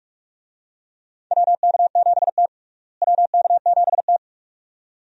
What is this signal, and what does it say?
Morse code, heard as a keyed tone, about 43 wpm, 700 Hz. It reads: WK7T WK7T
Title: USA Call Signs - 4 Characters - Repeated 2x - 45wpm